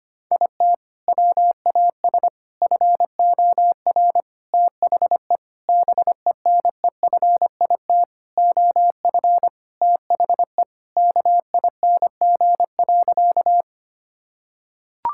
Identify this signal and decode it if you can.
Morse code, heard as a keyed tone, about 25 wpm, 700 Hz. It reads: IT WAS FOR THE BENEFIT OF THE KING.
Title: Sentences from Top 1000 Words - Review Part 6 - 25wpm